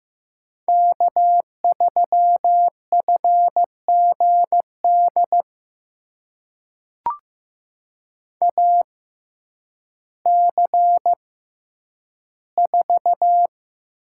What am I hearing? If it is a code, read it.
Morse code, heard as a keyed tone, about 15 wpm, spaced as for 9 wpm, 700 Hz. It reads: K3FGD AC4
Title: USA Call Signs - 5 Characters - 6x Character Spacing - 15wpm